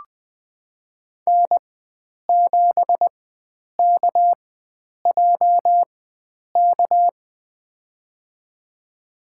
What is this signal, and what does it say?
Morse code, heard as a keyed tone, about 20 wpm, spaced as for 9 wpm, 700 Hz. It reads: N7KJK
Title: USA Call Signs - 5 Characters - 4x Character Spacing - 20wpm